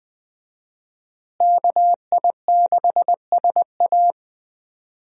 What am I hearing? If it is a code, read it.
Morse code, heard as a keyed tone, about 20 wpm, 700 Hz. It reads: KI6SA
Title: USA Call Signs - 5 Characters - Repeated 2x - 20wpm